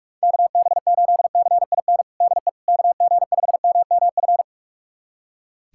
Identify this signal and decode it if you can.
Morse code, heard as a keyed tone, about 41 wpm, 690 Hz. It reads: KB8CIN DE KG5MMF
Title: USA Callsign DE Callsign - 6 Characters - 45wpm